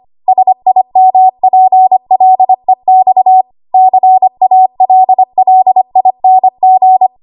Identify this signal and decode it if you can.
Morse code, heard as a keyed tone, about 25 wpm, 750 Hz. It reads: SIMPLEX CALLING
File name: Morse / 25WPM